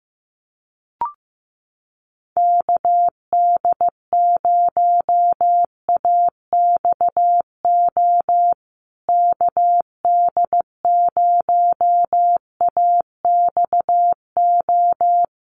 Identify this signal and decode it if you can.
Morse code, heard as a keyed tone, about 15 wpm, 700 Hz. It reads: KD0AXO KD0AXO
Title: USA Call Signs Repeated 2x 15wpm